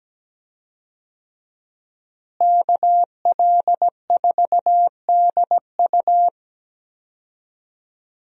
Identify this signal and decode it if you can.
Morse code, heard as a keyed tone, about 17 wpm, 700 Hz. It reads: KL4DU
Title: USA Call Signs - 5 Characters - Repeated 3x - 17wpm